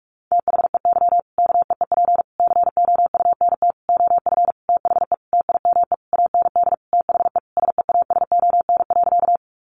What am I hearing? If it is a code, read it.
Morse code, heard as a keyed tone, about 44 wpm, 700 Hz. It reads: THEY KEEP COUNT OF THE TIME AND THE SEASON.
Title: Sentences from Top 1500 Words 45wpm